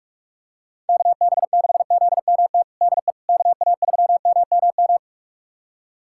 Morse code, heard as a keyed tone, about 45 wpm, 700 Hz. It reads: KB6ZMT DE KA3MMM
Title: USA Callsign DE Callsign 45wpm